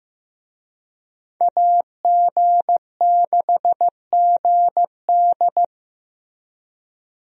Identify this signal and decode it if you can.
Morse code, heard as a keyed tone, about 15 wpm, 700 Hz. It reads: AG6GD
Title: USA Call Signs - 5 Characters - Repeated 2x - 15wpm